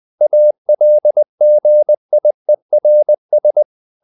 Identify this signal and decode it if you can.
Morse code, heard as a keyed tone, about 20 wpm, 600 Hz. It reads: ALGIERS
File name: CW / Practice / NationalCapitals20